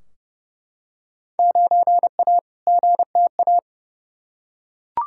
Morse code, heard as a keyed tone, about 29 wpm, 700 Hz. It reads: 9A GTA
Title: Field Day Exchanges Encore - 30wpm